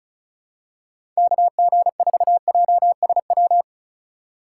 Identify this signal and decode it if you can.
Morse code, heard as a keyed tone, about 34 wpm, 700 Hz. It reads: KG4JSW
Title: USA Call Signs - 6 Characters - 2x Character Spacing - 35wpm